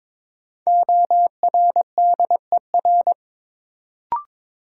Morse code, heard as a keyed tone, about 22 wpm, 700 Hz. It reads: ORDER
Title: Sentences from Top 400 Words - New Word Drill - 22wpm